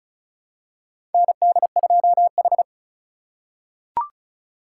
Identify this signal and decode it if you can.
Morse code, heard as a keyed tone, about 36 wpm, 700 Hz. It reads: ND2H
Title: USA Call Signs - 3x Character Spacing - 35wpm